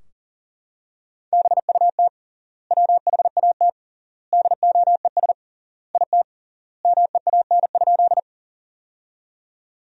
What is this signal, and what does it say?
Morse code, heard as a keyed tone, about 40 wpm, 700 Hz. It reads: BUT WHAT DOES IT MEAN?
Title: 3x Word-Spacing - Sentences from Top 200 Words - 40wpm